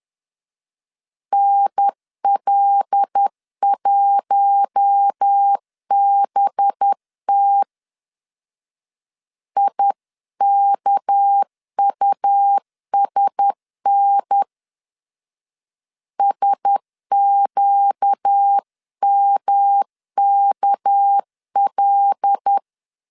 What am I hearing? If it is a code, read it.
Morse code, heard as a keyed tone, about 10 wpm, 800 Hz. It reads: NL1BT IKUSN SQMKL
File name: LEZIONI / 6 PKX0 / 6 PK 10wpm 2436